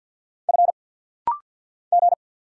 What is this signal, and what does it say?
Morse code, heard as a keyed tone, about 50 wpm, 690 Hz. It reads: F G
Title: Single Letter-Number Alt - Rapid-Fire 50wpm